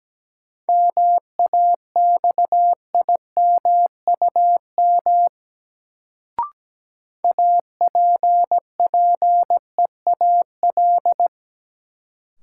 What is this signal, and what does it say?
Morse code, heard as a keyed tone, about 17 wpm, 700 Hz. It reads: MAXIMUM APPEAL
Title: Sentences from Top 1300 Words - New Word Drill - 17wpm